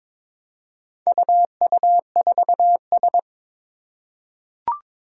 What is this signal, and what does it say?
Morse code, heard as a keyed tone, about 22 wpm, 700 Hz. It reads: UU4S
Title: Four Challenging Letter-Number Combinations 22wpm